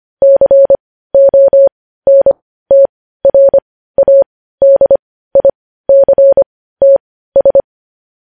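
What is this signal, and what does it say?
Morse code, heard as a keyed tone, about 25 wpm, spaced as for 15 wpm, 550 Hz. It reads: CONTRADICTS
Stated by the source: News headlines in Morse code at 15 words per minute, characters sent at 25 WPM.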